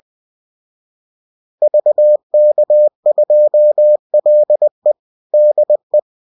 Morse code, heard as a keyed tone, about 20 wpm, 600 Hz. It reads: VK2LE DE